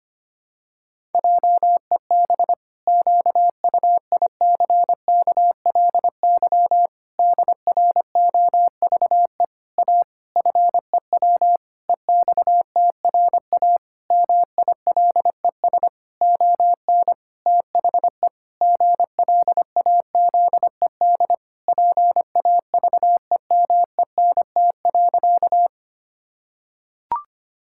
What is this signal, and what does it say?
Morse code, heard as a keyed tone, about 25 wpm, 700 Hz. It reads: JEB QUICKLY DROVE A FEW EXTRA MILES ON THE GLAZED PAVEMENT.
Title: Pangram Challenge v2 - 25wpm